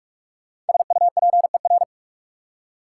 Morse code, heard as a keyed tone, about 57 wpm, 700 Hz. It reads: SUPER